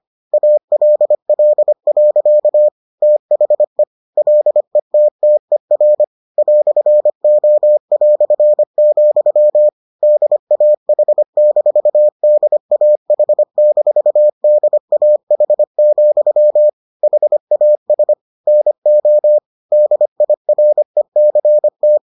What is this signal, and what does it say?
Morse code, heard as a keyed tone, about 25 wpm, 600 Hz. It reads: ALL. THE LETTER "O", DAH-DAH-DAH, HAS NO DIRECT